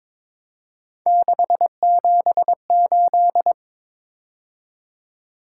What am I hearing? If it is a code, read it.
Morse code, heard as a keyed tone, about 22 wpm, 700 Hz. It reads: 678
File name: practice / 22wpm / Three Letter-Number Combinations - 22wpm